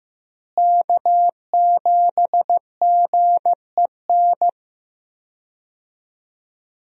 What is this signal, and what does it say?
Morse code, heard as a keyed tone, about 15 wpm, 700 Hz. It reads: K7GEN
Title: USA Call Signs - 5 Characters - Repeated 3x - 15wpm